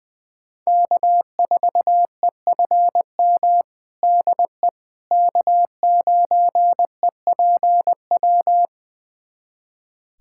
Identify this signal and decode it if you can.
Morse code, heard as a keyed tone, about 20 wpm, 700 Hz. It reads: K4EFM DE K9EPW
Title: USA Callsign DE Callsign - 5 Characters - 20wpm